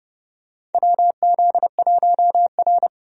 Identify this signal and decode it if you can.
Morse code, heard as a keyed tone, about 29 wpm, 700 Hz. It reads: WZ1R
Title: USA Call Signs - 4 Characters - 30wpm